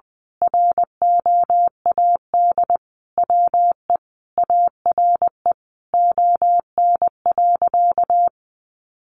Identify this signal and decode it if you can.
Morse code, heard as a keyed tone, about 19 wpm, 700 Hz. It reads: ROAD WE ARE ON.
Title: Sentences from Top 1100 Words 20wpm